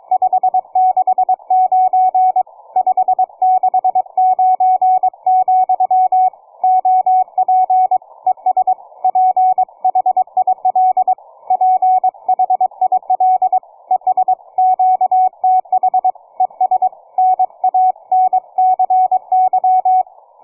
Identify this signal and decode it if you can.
Morse code, heard as a keyed tone, about 23 wpm, 740 Hz. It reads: 569 569, OP ES PHIL PHIL ES QTH ES NANCY
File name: cw forum / 9m2ax 160m1829 22h14